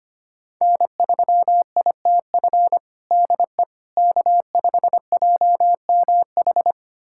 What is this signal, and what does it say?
Morse code, heard as a keyed tone, about 25 wpm, 700 Hz. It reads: N3ITF DE K5JMH
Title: USA Callsign DE Callsign - 5 Characters - 25wpm